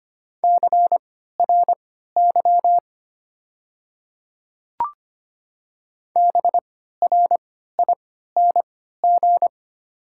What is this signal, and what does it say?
Morse code, heard as a keyed tone, about 25 wpm, spaced as for 14 wpm, 700 Hz. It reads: CRY BRING
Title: 3x Character Spacing - Top 500 Words - 25wpm